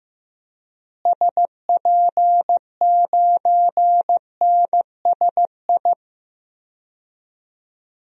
Morse code, heard as a keyed tone, about 15 wpm, 700 Hz. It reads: SP9NSI
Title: DXCC-100 Call Signs - 6 Characters - Repeated 2x - 15wpm